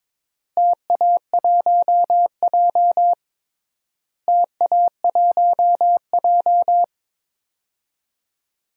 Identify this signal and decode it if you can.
Morse code, heard as a keyed tone, about 22 wpm, 700 Hz. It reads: TA1J TA1J
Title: DXCC-100 Call Signs - 4 Characters - Repeated 3x - 22wpm